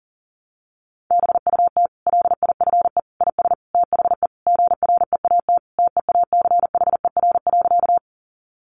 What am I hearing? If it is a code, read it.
Morse code, heard as a keyed tone, about 40 wpm, 700 Hz. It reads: BUT LIFE IS THE GREAT TEACHER.
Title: Sentences from Top 900 Words 40wpm